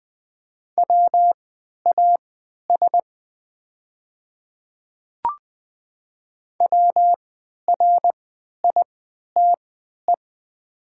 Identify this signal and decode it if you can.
Morse code, heard as a keyed tone, about 20 wpm, spaced as for 11 wpm, 700 Hz. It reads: WAS WRITE
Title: 3x Character Spacing - Top 100 Words - 20wpm